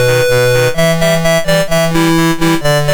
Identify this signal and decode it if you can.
There is no code.